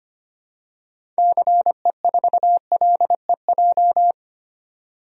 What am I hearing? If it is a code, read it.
Morse code, heard as a keyed tone, about 25 wpm, 700 Hz. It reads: CE4LEJ
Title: DXCC-100 Call Signs - 6 Characters - 25wpm